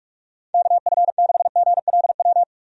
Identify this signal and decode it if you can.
Morse code, heard as a keyed tone, about 46 wpm, 700 Hz. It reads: KF6GLW
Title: USA Call Signs - 6 Characters - 45wpm